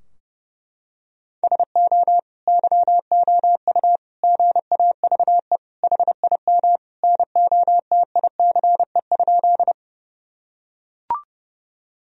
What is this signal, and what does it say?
Morse code, heard as a keyed tone, about 32 wpm, 700 Hz. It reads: SO YOU GAVE HIM NOTICE?
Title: Sentences from Top 1000 Words - Review Part 2 - 30wpm